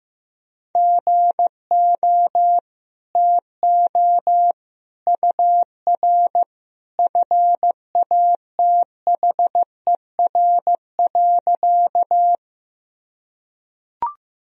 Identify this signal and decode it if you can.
Morse code, heard as a keyed tone, about 15 wpm, 700 Hz. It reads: GO TO UR FATHER.
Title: Sentences from Top 300 Words and Abbreviations 15wpm